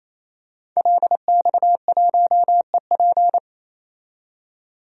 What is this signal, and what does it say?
Morse code, heard as a keyed tone, about 28 wpm, 700 Hz. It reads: LX1EP
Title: DXCC-100 Call Signs - 5 Characters - 28wpm